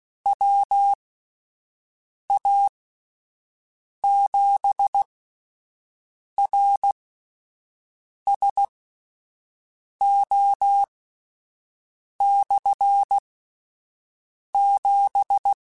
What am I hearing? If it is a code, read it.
Morse code, heard as a keyed tone, about 16 wpm, spaced as for 5 wpm, 800 Hz. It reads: WA7RSO/7
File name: Combinations / Combinations-mp3s / Combinations@5